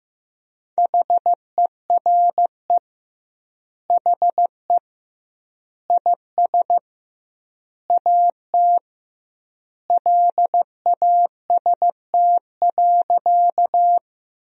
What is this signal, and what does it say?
Morse code, heard as a keyed tone, about 15 wpm, 700 Hz. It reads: HERE HE IS AT LAST.
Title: 2x Word-Spacing - Sentences from Top 300 Words - 15wpm